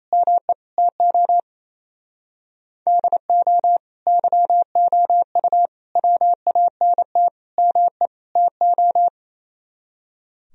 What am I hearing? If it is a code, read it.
Morse code, heard as a keyed tone, about 35 wpm, 700 Hz. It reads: ME TO DO YOU WANT ME TO
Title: Speed-Racing - Sets of 5 Words 28wpm